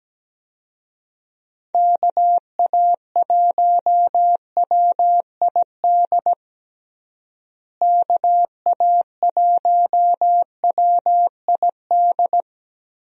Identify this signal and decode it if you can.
Morse code, heard as a keyed tone, about 17 wpm, 700 Hz. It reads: KA1WID KA1WID
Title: USA Call Signs - 6 Characters - Repeated 3x - 17wpm